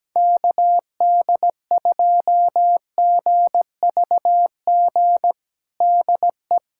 Morse code, heard as a keyed tone, about 17 wpm, 700 Hz. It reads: KD2GVG DE
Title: USA Callsign DE Callsign - 6 Characters - 17wpm